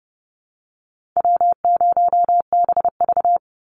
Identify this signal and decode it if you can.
Morse code, heard as a keyed tone, about 30 wpm, 700 Hz. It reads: W0BV
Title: Easy Call Signs Repeated 3x - 30wpm